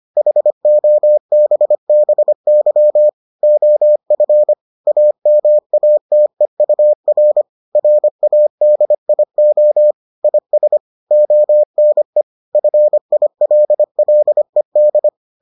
Morse code, heard as a keyed tone, about 25 wpm, 600 Hz. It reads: HOBBY OF AMATEUR RADIO IS ONE FILLED